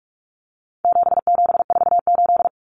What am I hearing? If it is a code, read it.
Morse code, heard as a keyed tone, about 44 wpm, 700 Hz. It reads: 7748
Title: Common Cut Numbers - 4-Digit 45wpm